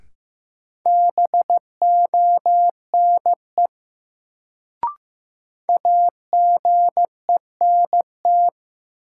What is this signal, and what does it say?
Morse code, heard as a keyed tone, about 15 wpm, 700 Hz. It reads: BONE AGENT
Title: Sentences from Top 1600 Words - New Word Drill - 15wpm